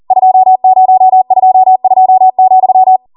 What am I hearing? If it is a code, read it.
Morse code, heard as a keyed tone, about 40 wpm, 750 Hz. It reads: 2022,